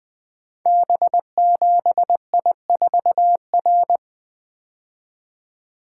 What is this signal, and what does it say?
Morse code, heard as a keyed tone, about 20 wpm, 700 Hz. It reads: B7I4R
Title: Code Groups - 5 Letter:Numbers - 20wpm